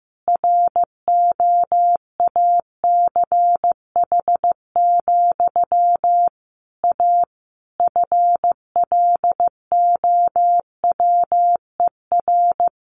Morse code, heard as a keyed tone, about 15 wpm, 700 Hz. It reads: ROACH, A FLOWER
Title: A Sound Of Thunder 15wpm